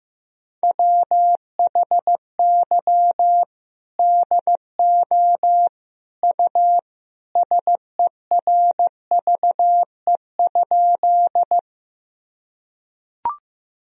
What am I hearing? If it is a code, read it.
Morse code, heard as a keyed tone, about 16 wpm, 700 Hz. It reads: WHY DO U SERVE?
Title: Sentences from Top 400 Words and Abbreviations 15wpm